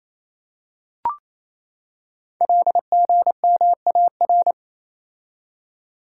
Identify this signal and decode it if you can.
Morse code, heard as a keyed tone, about 27 wpm, 690 Hz. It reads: LGMAR
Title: Code Groups - 5 Letters - 28wpm